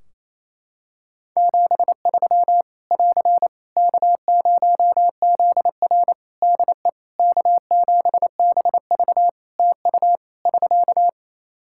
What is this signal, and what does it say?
Morse code, heard as a keyed tone, about 27 wpm, 700 Hz. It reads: 73 <AR> K0ZR DE K7BV TU <SK>
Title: K1USN SST QSO Practice 28wpm